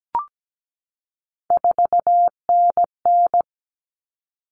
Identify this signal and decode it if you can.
Morse code, heard as a keyed tone, about 17 wpm, 700 Hz. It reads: 4NN
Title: Common Cut Numbers - 3-Digit 17wpm